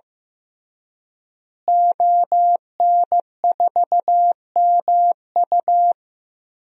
Morse code, heard as a keyed tone, about 15 wpm, 700 Hz. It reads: ON4MU
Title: DXCC-100 Call Signs - 5 Characters - Repeated 3x - 15wpm